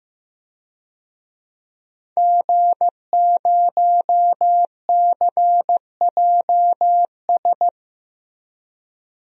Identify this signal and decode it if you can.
Morse code, heard as a keyed tone, about 15 wpm, 700 Hz. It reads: G0CJS
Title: DXCC-100 Call Signs - 5 Characters - Repeated 2x - 15wpm